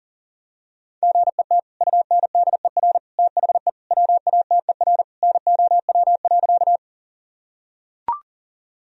Morse code, heard as a keyed tone, about 40 wpm, 700 Hz. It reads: GET UNDER THE WATER NOW.
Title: Sentences from Top 1000 Words - Review Part 1 - 40wpm